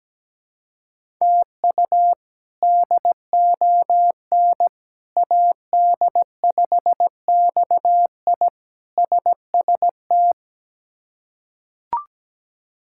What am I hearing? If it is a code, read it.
Morse code, heard as a keyed tone, about 17 wpm, 700 Hz. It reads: TU DON AD5XI SST
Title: K1USN SST QSO Practice - 17wpm